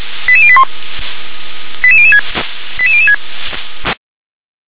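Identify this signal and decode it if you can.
Five-tone selective call (ZVEI), heard as quick runs of five tones, tones of about 70 ms. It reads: 88081 88006 88006
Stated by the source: ZVEI five-tone selective call received by SDR.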